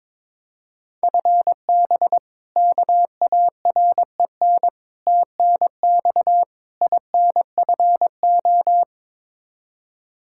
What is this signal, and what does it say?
Morse code, heard as a keyed tone, about 22 wpm, 700 Hz. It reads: FB KAREN TNX INFO
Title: K1USN SST QSO Practice 22wpm